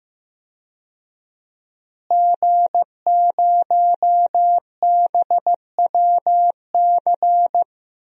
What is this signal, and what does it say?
Morse code, heard as a keyed tone, about 15 wpm, 700 Hz. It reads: G0BWC
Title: DXCC-100 Call Signs - 5 Characters - Repeated 3x - 15wpm